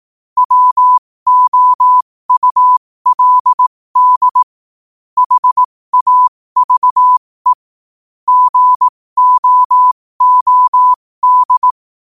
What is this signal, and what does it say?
Morse code, heard as a keyed tone, about 18 wpm, spaced as for 15 wpm, 1000 Hz. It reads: WOULD HAVE GOOD
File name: qotdfiles / 15wpm